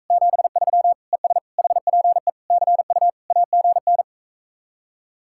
Morse code, heard as a keyed tone, about 44 wpm, 700 Hz. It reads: 73 ES HPE CU AGN